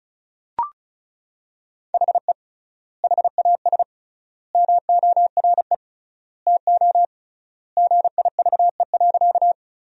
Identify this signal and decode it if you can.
Morse code, heard as a keyed tone, about 36 wpm, 700 Hz. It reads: HE HAS MORE TO GIVE.
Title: 3x Word-Spacing - Sentences from Top 200 Words - 35wpm